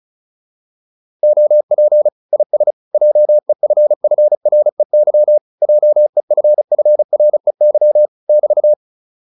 Morse code, heard as a keyed tone, about 34 wpm, 600 Hz. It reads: OP IS JEFFREY JEFFREY <BT>